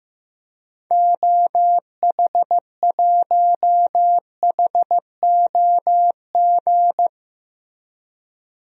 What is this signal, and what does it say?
Morse code, heard as a keyed tone, about 15 wpm, 700 Hz. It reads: OH1HOG
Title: DXCC-100 Call Signs - 6 Characters - Repeated 2x - 15wpm